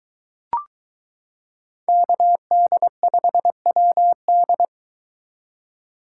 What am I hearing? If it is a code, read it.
Morse code, heard as a keyed tone, about 23 wpm, 700 Hz. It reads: KD5WD